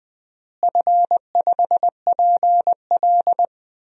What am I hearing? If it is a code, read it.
Morse code, heard as a keyed tone, about 21 wpm, 700 Hz. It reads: F5PL
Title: DXCC-100 Call Signs - 4 Characters - Repeated 2x - 20wpm